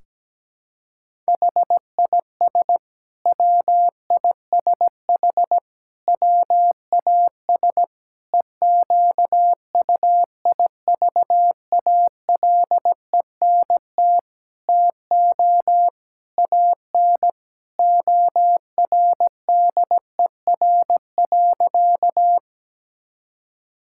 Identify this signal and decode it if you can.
Morse code, heard as a keyed tone, about 17 wpm, 700 Hz. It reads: HIS WISH WAS EQUIVALENT TO AN ORDER.
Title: Sentences from Top 1600 Words 17wpm